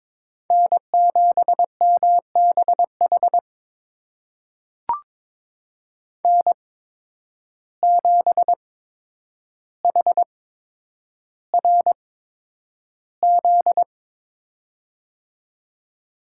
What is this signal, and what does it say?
Morse code, heard as a keyed tone, about 21 wpm, spaced as for 9 wpm, 700 Hz. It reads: N7MBH N7HRZ
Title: USA Call Signs - 5 Characters - 8x Character Spacing - 22wpm